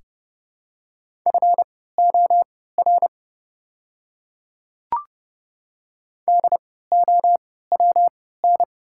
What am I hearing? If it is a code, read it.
Morse code, heard as a keyed tone, about 31 wpm, spaced as for 17 wpm, 690 Hz. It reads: FOR DOWN
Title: 3x Character Spacing - Top 300 Words - 30wpm